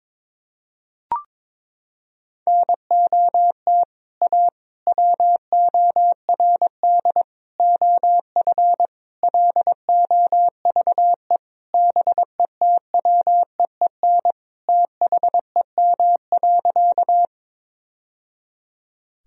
Morse code, heard as a keyed tone, about 22 wpm, 700 Hz. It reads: NOT A WORD OF LOVE BETWEEN THEM.
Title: Sentences from Top 1000 Words - Review Part 1 - 22wpm